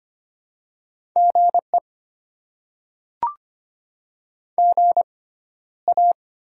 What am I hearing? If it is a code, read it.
Morse code, heard as a keyed tone, about 25 wpm, spaced as for 13 wpm, 700 Hz. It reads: GE GA
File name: Top 20 QSO Elements - 6x Character Spacing - 25wpm